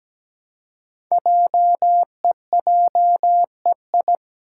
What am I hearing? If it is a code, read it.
Morse code, heard as a keyed tone, about 17 wpm, 700 Hz. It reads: JEJEI
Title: Code Groups - 5 Letter:Numbers - 17wpm